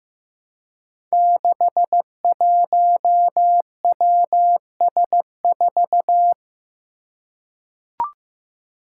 Morse code, heard as a keyed tone, about 15 wpm, 700 Hz. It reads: 61WS4
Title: Five Challenging Letter-Number Combinations 15wpm